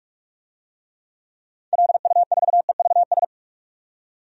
Morse code, heard as a keyed tone, about 46 wpm, 700 Hz. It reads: LU4EVS
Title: DXCC-100 Call Signs - 6 Characters - 45wpm